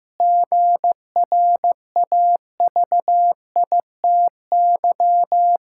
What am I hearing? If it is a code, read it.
Morse code, heard as a keyed tone, about 15 wpm, 700 Hz. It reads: GRAVITY
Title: Pangram Challenge v2 - 15wpm